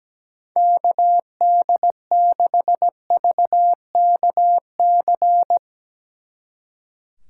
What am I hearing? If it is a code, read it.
Morse code, heard as a keyed tone, about 19 wpm, 700 Hz. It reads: KD6VKC
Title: Speed Racing - USA 2x3 Call Signs 17wpm